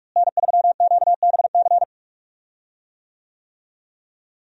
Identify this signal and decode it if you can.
Morse code, heard as a keyed tone, about 46 wpm, 700 Hz. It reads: N3QBC
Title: USA Call Signs - 5 Characters - Repeated 2x - 45wpm